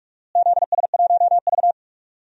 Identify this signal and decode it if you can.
Morse code, heard as a keyed tone, about 44 wpm, 700 Hz. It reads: ZS1V
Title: DXCC-100 Call Signs - 4 Characters - Repeated 2x - 45wpm